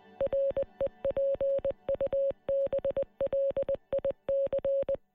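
Morse code, heard as a keyed tone, about 20 wpm, 550 Hz. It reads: REPUBLIC
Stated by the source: Morse code message re-enacting the 1916 Rising broadcast by Marconi operator David Burke, transmitted on air by 37 radio stations across the state.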